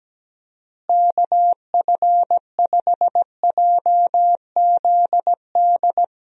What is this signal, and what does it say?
Morse code, heard as a keyed tone, about 17 wpm, 700 Hz. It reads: KF5JZD
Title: USA Call Signs - 6 Characters - Repeated 2x - 17wpm